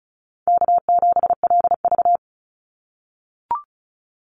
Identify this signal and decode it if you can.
Morse code, heard as a keyed tone, about 34 wpm, 700 Hz. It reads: K7LV
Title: Call Signs - 35wpm